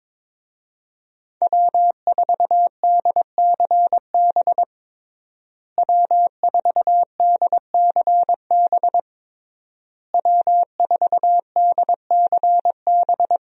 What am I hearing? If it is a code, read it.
Morse code, heard as a keyed tone, about 22 wpm, 700 Hz. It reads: W4DCB W4DCB W4DCB
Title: USA Call Signs - 5 Characters - Repeated 3x - 22wpm